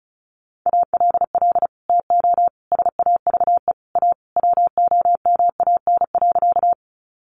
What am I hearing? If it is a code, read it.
Morse code, heard as a keyed tone, about 34 wpm, 700 Hz. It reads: ALL TO SAVE A WOMAN.
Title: Sentences from Top 1100 Words 35wpm